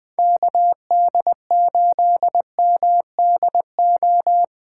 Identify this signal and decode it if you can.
Morse code, heard as a keyed tone, about 20 wpm, 700 Hz. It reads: KD8MDO
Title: USA Call Signs - 6 Characters - 2x Character Spacing - 20wpm